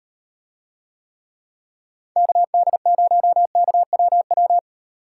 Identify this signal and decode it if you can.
Morse code, heard as a keyed tone, about 38 wpm, 700 Hz. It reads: KD0KWW